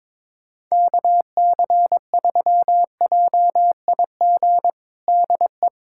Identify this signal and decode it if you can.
Morse code, heard as a keyed tone, about 22 wpm, 700 Hz. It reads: KC3JIG DE
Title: USA Callsign DE Callsign 22wpm